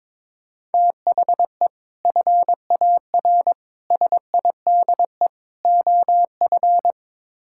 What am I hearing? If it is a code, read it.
Morse code, heard as a keyed tone, about 22 wpm, 700 Hz. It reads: THE FAR SIDE OF
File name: practice / 22wpm / Build A Phrase - 5-Word Phrases - 22wpm